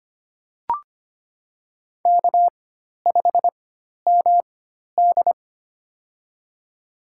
Morse code, heard as a keyed tone, about 25 wpm, spaced as for 12 wpm, 700 Hz. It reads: K5MD
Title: USA Call Signs - 4 Characters - 4x Character Spacing - 25wpm